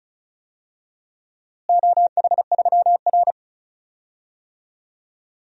Morse code, heard as a keyed tone, about 36 wpm, 700 Hz. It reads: OH3R
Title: DXCC-100 Call Signs - 4 Characters - Repeated 2x - 35wpm